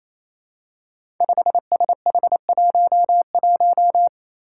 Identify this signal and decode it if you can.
Morse code, heard as a keyed tone, about 27 wpm, 700 Hz. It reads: 5SH11
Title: Five Challenging Letter-Number Combinations 28wpm